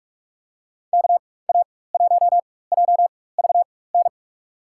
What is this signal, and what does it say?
Morse code, heard as a keyed tone, about 42 wpm, spaced as for 21 wpm, 700 Hz. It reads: KA1JVN
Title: USA Call Signs - 6 Characters - 4x Character Spacing - 45wpm